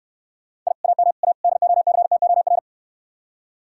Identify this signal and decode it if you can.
Morse code, heard as a keyed tone, about 69 wpm, 700 Hz. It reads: I AM A DOCTOR